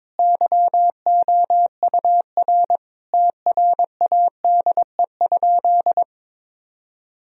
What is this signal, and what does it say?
Morse code, heard as a keyed tone, about 22 wpm, 700 Hz. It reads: YOUR TRADE?
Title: Sentences from Top 1000 Words - Review Part 4 - 22wpm